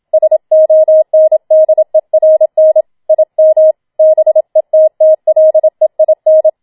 Morse code, heard as a keyed tone, about 27 wpm, 630 Hz. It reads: SONDERN IM BETTLEIN